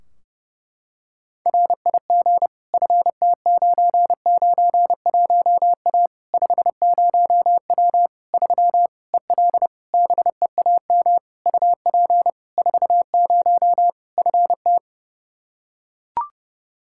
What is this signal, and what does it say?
Morse code, heard as a keyed tone, about 30 wpm, 700 Hz. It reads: RIG FT991A 50W 3 EL BEAM UP 40 FT
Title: Practice QSO - Exchange 2 - Info - 30wpm